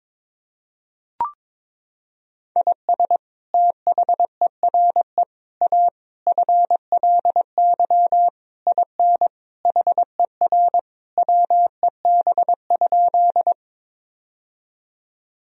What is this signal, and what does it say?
Morse code, heard as a keyed tone, about 21 wpm, 700 Hz. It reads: IS THERE A FLY IN HER WEB?
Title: Sentences from Top 1000 Words - Review Part 4 - 22wpm